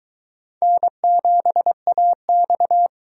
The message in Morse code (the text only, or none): N7AX